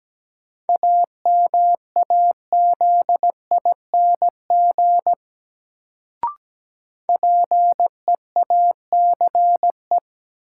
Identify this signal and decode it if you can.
Morse code, heard as a keyed tone, about 17 wpm, 700 Hz. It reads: AMAZING PEACE
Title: Sentences from Top 1500 Words - New Word Drill Alt - 17wpm